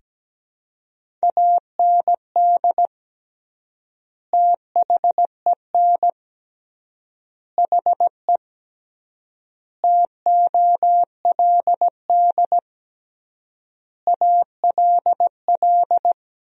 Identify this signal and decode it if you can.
Morse code, heard as a keyed tone, about 17 wpm, 700 Hz. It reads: AND THEN HE TOLD ALL
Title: 3x Word-Spacing - Sentences from Top 400 Words - 17wpm